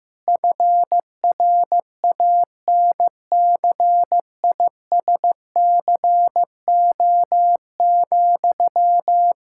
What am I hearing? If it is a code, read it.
Morse code, heard as a keyed tone, about 15 wpm, 700 Hz. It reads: FRANCISCO,
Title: Practice QSO - Exchange 1 - Beginning - 15wpm